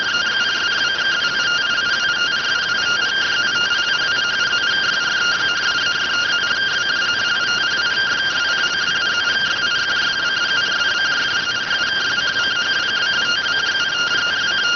Two-tone teleprinter signal, about 170 Hz shift.